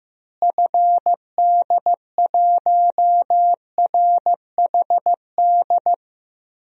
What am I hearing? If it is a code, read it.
Morse code, heard as a keyed tone, about 15 wpm, 700 Hz. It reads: FD1RHD
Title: DXCC-100 Call Signs - 6 Characters - Repeated 2x - 15wpm